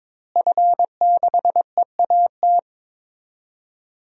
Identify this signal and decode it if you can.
Morse code, heard as a keyed tone, about 21 wpm, 700 Hz. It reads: F6EAT
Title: DXCC-100 Call Signs - 5 Characters - Repeated 3x - 22wpm